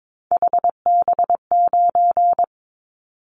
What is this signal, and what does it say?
Morse code, heard as a keyed tone, about 22 wpm, 700 Hz. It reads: HB9
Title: Callsign Prefixes for Top 25 DXCC - 22wpm